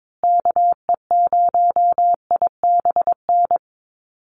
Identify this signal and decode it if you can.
Morse code, heard as a keyed tone, about 22 wpm, 700 Hz. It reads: KE0IBN
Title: USA Call Signs Repeated 3x 22wpm